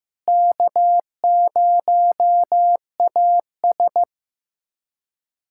Morse code, heard as a keyed tone, about 15 wpm, 700 Hz. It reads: K0AS
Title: USA Callsign DE Callsign - 4 Characters - 15wpm